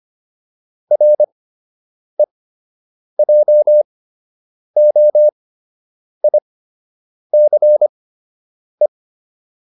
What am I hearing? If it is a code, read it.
Morse code, heard as a keyed tone, about 25 wpm, spaced as for 8 wpm, 600 Hz. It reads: REJOICE